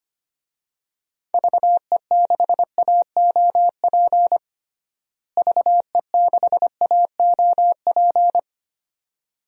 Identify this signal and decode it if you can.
Morse code, heard as a keyed tone, about 25 wpm, 700 Hz. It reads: VE6AOP VE6AOP
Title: DXCC-100 Call Signs - 6 Characters - Repeated 2x - 25wpm